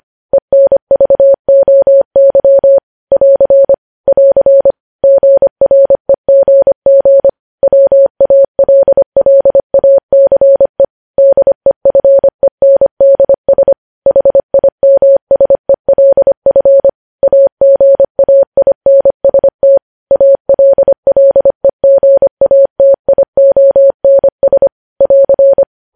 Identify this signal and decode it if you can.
Morse code, heard as a keyed tone, about 25 wpm, 550 Hz. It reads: ENVOY <AR> <AR> GREGG WALLACE DEFENDS HIMSELF AGAINST ALLEGATIONS <AR>
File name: podcast / News headlines-1x-25x25